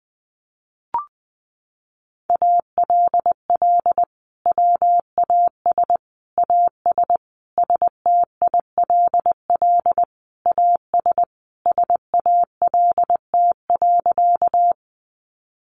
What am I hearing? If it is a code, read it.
Morse code, heard as a keyed tone, about 20 wpm, 700 Hz. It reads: ALL WAS AS STILL AS SALT.
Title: Sentences from Top 800 Words 20wpm